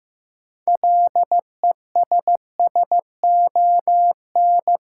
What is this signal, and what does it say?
Morse code, heard as a keyed tone, about 16 wpm, 700 Hz. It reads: LESSON
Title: Sentences from Top 1200 Words - New Word Drill - 15wpm